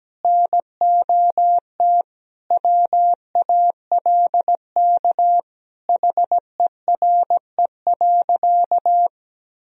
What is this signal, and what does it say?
Morse code, heard as a keyed tone, about 17 wpm, 700 Hz. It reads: NOT WALK HERE.
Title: Sentences from Top 1000 Words - Review Part 1 - 17wpm